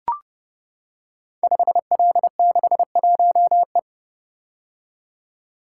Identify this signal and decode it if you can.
Morse code, heard as a keyed tone, about 32 wpm, 700 Hz. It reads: 5L61E
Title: Code Groups - 5 Letter:Numbers - 30wpm